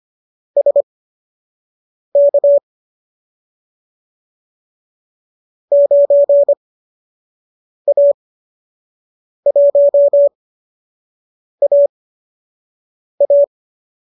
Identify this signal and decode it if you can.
Morse code, heard as a keyed tone, about 25 wpm, spaced as for 6 wpm, 570 Hz. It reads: SK 9A1AA
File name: qso012 06